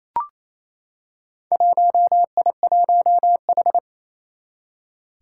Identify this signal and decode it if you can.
Morse code, heard as a keyed tone, about 28 wpm, 700 Hz. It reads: 1I1H
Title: Four Challenging Letter-Number Combinations 28wpm